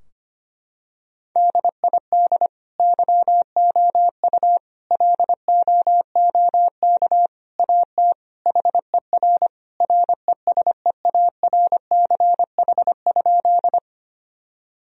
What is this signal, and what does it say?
Morse code, heard as a keyed tone, about 25 wpm, 700 Hz. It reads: DID YOU LOOK AT HER RESEARCH?
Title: Sentences from Top 1000 Words - Review Part 4 - 25wpm